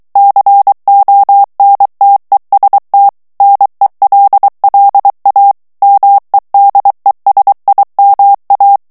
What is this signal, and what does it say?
Morse code, heard as a keyed tone, about 23 wpm, 800 Hz. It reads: CONTEST NELLA MEDESIMA
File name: ITALIEN texte clair 2 vitesse 22 mots minute